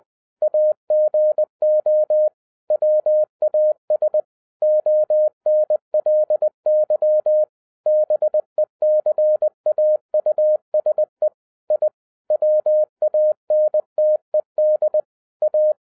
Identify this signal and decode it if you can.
Morse code, heard as a keyed tone, about 20 wpm, 610 Hz. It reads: AGO WAS ONLY BECAUSE I WANTED A
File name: 2012-01-mp3 / n05-CAME BACK TO HAM RADIO AFTER 30 YEARS OF INTERVAL-20wpm